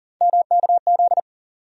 Morse code, heard as a keyed tone, about 40 wpm, 700 Hz. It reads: MKZ
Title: USA Call Sign Postfixes - 3 Letters 40wpm